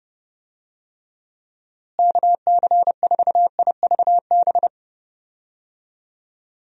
Morse code, heard as a keyed tone, about 30 wpm, 700 Hz. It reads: KC4IVB
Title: USA Call Signs - 6 Characters - Repeated 3x - 30wpm